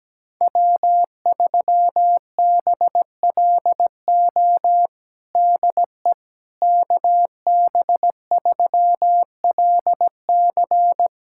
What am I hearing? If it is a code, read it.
Morse code, heard as a keyed tone, about 17 wpm, 700 Hz. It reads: W3BLO DE KB3LC
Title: USA Callsign DE Callsign - 5 Characters - 17wpm